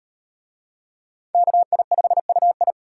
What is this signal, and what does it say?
Morse code, heard as a keyed tone, about 38 wpm, 700 Hz. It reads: KI5UI